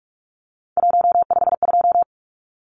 Morse code, heard as a keyed tone, about 44 wpm, 700 Hz. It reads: 152
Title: Numbers - 3-Digit 45wpm